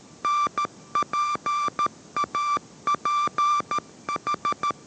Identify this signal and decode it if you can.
Morse code, heard as a keyed tone, about 13 wpm, 1180 Hz. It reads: NPAPH